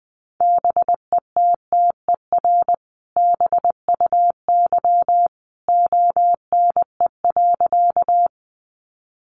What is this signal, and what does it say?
Morse code, heard as a keyed tone, about 20 wpm, 700 Hz. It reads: BETTER BUY ONE.
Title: Sentences from Top 700 Words 20wpm